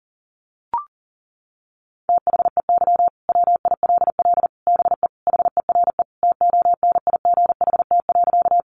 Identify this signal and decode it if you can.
Morse code, heard as a keyed tone, about 40 wpm, 700 Hz. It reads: THEY WILL BE HERE TONIGHT.
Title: Sentences from Top 600 Words 40wpm